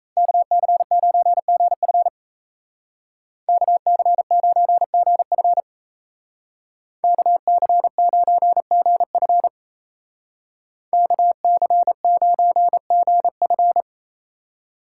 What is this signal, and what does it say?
Morse code, heard as a keyed tone, about 34 wpm, 700 Hz. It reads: KC9GF KC9GF KC9GF KC9GF
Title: Speed Racing - USA 2x2 Call Signs 28wpm